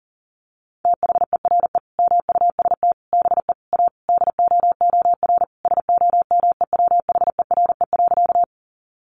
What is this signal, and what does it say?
Morse code, heard as a keyed tone, about 40 wpm, 700 Hz. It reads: THERE MUST BE A DOOR SOMEWHERE.
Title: Sentences from Top 1400 Words 40wpm